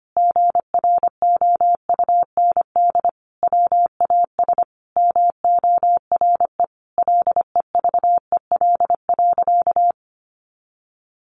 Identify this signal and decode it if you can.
Morse code, heard as a keyed tone, about 25 wpm, 700 Hz. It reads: GROUND WAS MORE LEVEL.
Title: Sentences from Top 700 Words 25wpm